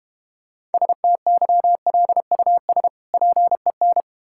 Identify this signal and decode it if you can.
Morse code, heard as a keyed tone, about 32 wpm, 700 Hz. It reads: STYLUS PEN